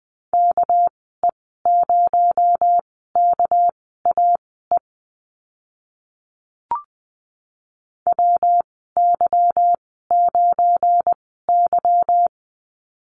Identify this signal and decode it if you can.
Morse code, heard as a keyed tone, about 20 wpm, spaced as for 14 wpm, 700 Hz. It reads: KE0KAE WY9Y
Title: 231 Lesson 37g - Letter x - Call Signs 20wpm